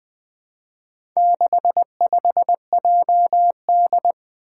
Morse code, heard as a keyed tone, about 20 wpm, 700 Hz. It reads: 65JD
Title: Four Challenging Letter-Number Combinations 20wpm